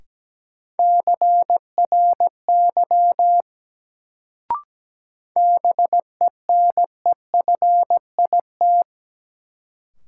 Morse code, heard as a keyed tone, about 17 wpm, 700 Hz. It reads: CRY BENEFIT
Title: Sets of 5 Words Encore - New Word Drill - 17wpm